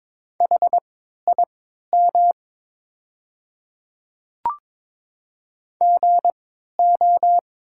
Morse code, heard as a keyed tone, about 23 wpm, spaced as for 13 wpm, 690 Hz. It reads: HIM GO
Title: 3x Character Spacing - Top 500 Words - 22wpm